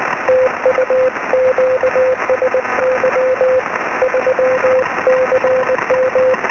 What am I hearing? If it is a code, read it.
Morse code, heard as a keyed tone, about 19 wpm, 520 Hz, with noise. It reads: TU QSY 3CM
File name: sm7lcb / 10-dl1sun-23cm